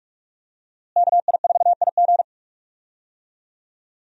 Morse code, heard as a keyed tone, about 46 wpm, 700 Hz. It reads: KI4IG
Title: USA Call Signs - 5 Characters - Repeated 3x - 45wpm